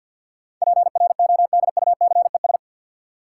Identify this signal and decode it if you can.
Morse code, heard as a keyed tone, about 50 wpm, 700 Hz. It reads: PRODUCES
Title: Sentences from Top 1000 Words - Review Part 6 - New Word Drill - 50wpm